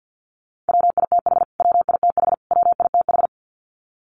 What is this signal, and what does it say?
Morse code, heard as a keyed tone, about 48 wpm, 700 Hz. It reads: WITH WITH WITH
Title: Top 500 Words Repeated 50wpm